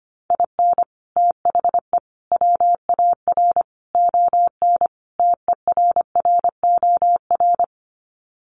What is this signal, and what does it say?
Morse code, heard as a keyed tone, about 25 wpm, 700 Hz. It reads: IN THE WAR ON TERROR
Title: Sets of 5 Words Encore2 25wpm